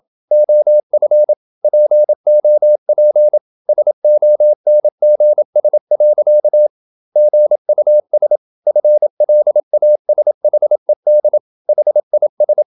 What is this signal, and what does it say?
Morse code, heard as a keyed tone, about 27 wpm, 600 Hz. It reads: OF POP SONGS. GUS FLASHED HIS
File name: Never 0027wpm